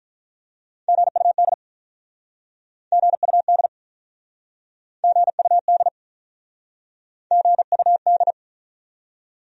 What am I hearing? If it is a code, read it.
Morse code, heard as a keyed tone, about 44 wpm, 700 Hz. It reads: GUD GUD GUD GUD